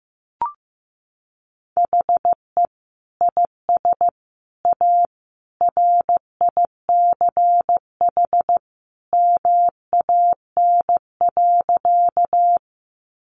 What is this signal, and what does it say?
Morse code, heard as a keyed tone, about 15 wpm, 700 Hz. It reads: HE IS A RICH MAN.
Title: Sentences from Top 1100 Words 15wpm